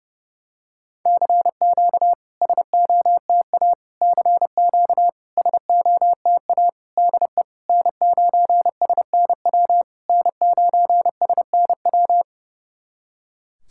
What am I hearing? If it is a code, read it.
Morse code, heard as a keyed tone, about 30 wpm, 700 Hz. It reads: CQ SOTA CQ SOTA DE N9SNW N9SNW
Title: SOTA QSO Practice 30wpm